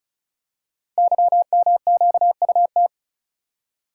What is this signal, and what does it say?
Morse code, heard as a keyed tone, about 35 wpm, 700 Hz. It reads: YMQUT
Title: Code Groups - 5 Letters - 35wpm